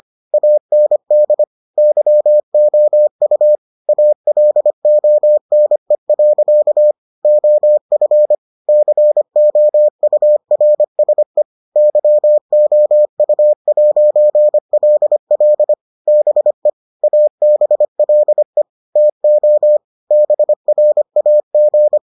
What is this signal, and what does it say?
Morse code, heard as a keyed tone, about 25 wpm, 600 Hz. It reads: AND YOU ALONE. OF COURSE YOU'LL BE ABLE TO BRAG